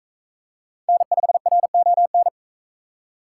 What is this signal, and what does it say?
Morse code, heard as a keyed tone, about 38 wpm, 700 Hz. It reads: N5RON